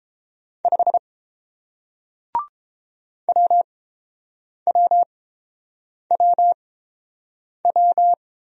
Morse code, heard as a keyed tone, about 30 wpm, spaced as for 7 wpm, 700 Hz. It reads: 5 WWWW